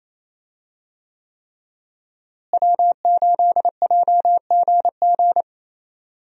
Morse code, heard as a keyed tone, about 28 wpm, 700 Hz. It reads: W8JGG
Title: USA Call Signs - 5 Characters - Repeated 3x - 28wpm